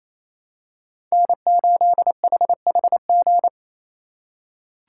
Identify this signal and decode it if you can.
Morse code, heard as a keyed tone, about 28 wpm, 700 Hz. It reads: N8HHG
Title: K1USN SST QSO Practice - 28wpm